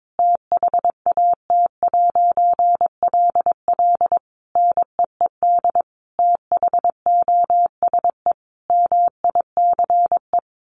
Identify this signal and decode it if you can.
Morse code, heard as a keyed tone, about 22 wpm, 700 Hz. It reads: THAT'LL NEED THOSE MICE
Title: A Sound Of Thunder 22wpm